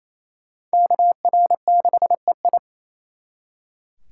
Morse code, heard as a keyed tone, about 35 wpm, 700 Hz. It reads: KR6EI